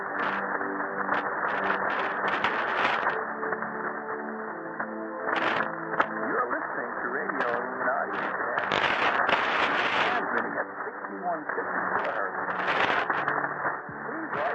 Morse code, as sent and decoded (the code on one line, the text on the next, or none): none